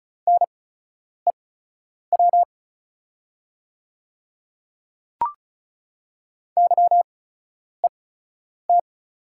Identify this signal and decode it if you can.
Morse code, heard as a keyed tone, about 36 wpm, spaced as for 10 wpm, 700 Hz. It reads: NEW YET
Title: 8x Character Spacing - Top 500 Words - 35wpm